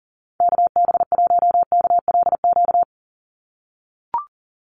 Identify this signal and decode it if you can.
Morse code, heard as a keyed tone, about 40 wpm, 700 Hz. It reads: KB1KLQ
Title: USA Call Signs 40wpm